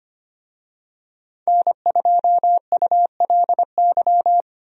Morse code, heard as a keyed tone, about 25 wpm, 700 Hz. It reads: N2ULY
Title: USA Call Signs - 5 Characters - 25wpm